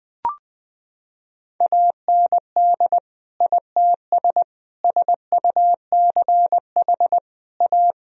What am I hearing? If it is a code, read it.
Morse code, heard as a keyed tone, about 20 wpm, 700 Hz. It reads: AND ITS SUCH A
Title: Sentences from Top 1000 Words - Review Part 2 - 20wpm